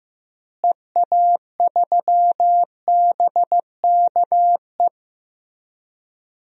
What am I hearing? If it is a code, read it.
Morse code, heard as a keyed tone, about 15 wpm, 700 Hz. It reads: EA3BKE
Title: DXCC-100 Call Signs - 6 Characters - Repeated 3x - 15wpm